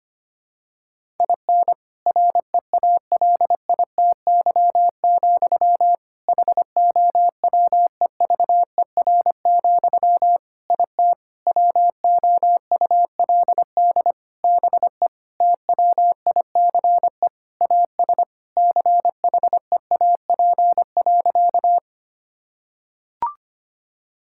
Morse code, heard as a keyed tone, about 25 wpm, 700 Hz. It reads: IN REALITY, HOWEVER, IT WOULD BE TWICE AS CHEAP.
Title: Sentences from Top 1600 Words 25wpm